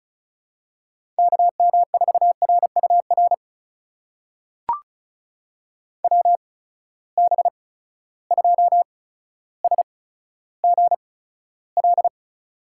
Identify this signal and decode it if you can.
Morse code, heard as a keyed tone, about 36 wpm, spaced as for 15 wpm, 700 Hz. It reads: KM4RUR WB2SGL
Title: USA Call Signs - 6 Characters - 8x Character Spacing - 35wpm